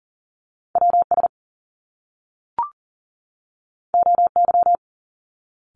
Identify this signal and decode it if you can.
Morse code, heard as a keyed tone, about 40 wpm, 700 Hz. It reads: WS OY
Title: Top Two Letter Combinations in Words 40wpm